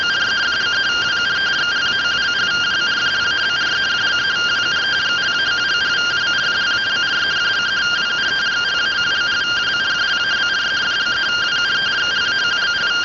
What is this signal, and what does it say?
Two-tone teleprinter signal, about 170 Hz shift.
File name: findings / axial iodine 10mm 184.gen-44 rtty rx